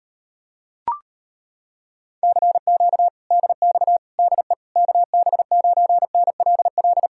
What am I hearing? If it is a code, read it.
Morse code, heard as a keyed tone, about 33 wpm, 700 Hz. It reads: CQ DX DE KB9NLL